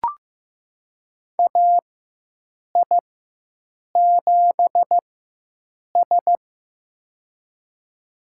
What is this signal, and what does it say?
Morse code, heard as a keyed tone, about 15 wpm, spaced as for 7 wpm, 700 Hz. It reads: AI7S
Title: USA Call Signs - 4 Characters - 4x Character Spacing - 15wpm